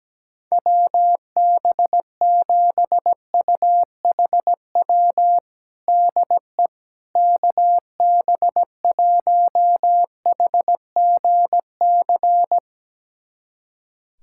Morse code, heard as a keyed tone, about 17 wpm, 700 Hz. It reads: WB7UHW DE KB1HGC
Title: USA Callsign DE Callsign 17wpm